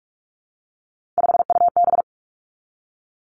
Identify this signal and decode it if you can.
Morse code, heard as a keyed tone, about 46 wpm, 700 Hz. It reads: 5UB
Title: Three Challenging Letter-Number Combinations 45wpm